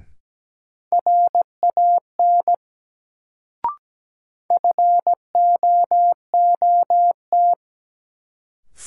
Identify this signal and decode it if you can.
Morse code, heard as a keyed tone, about 17 wpm, 700 Hz. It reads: RAN FOOT
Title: Sentences from Top 500 Words - New Word Drill - 17wpm